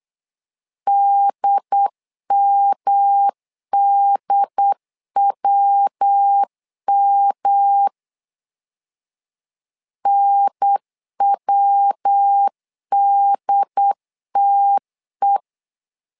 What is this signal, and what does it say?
Morse code, heard as a keyed tone, about 8 wpm, 800 Hz. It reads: DMDWM NWDTE
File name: LEZIONI / 3 MWRO / 3 MW 8wpm 2501